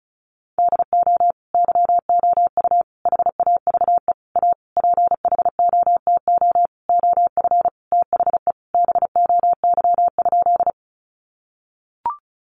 Morse code, heard as a keyed tone, about 34 wpm, 700 Hz. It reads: DO YOU HAVE A PHOTO OF THE BOY?
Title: Sentences from Top 1200 Words 35wpm